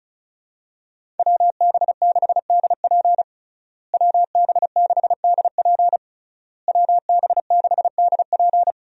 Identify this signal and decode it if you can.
Morse code, heard as a keyed tone, about 34 wpm, 700 Hz. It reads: WB6DP WB6DP WB6DP
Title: USA Call Signs - 5 Characters - Repeated 3x - 35wpm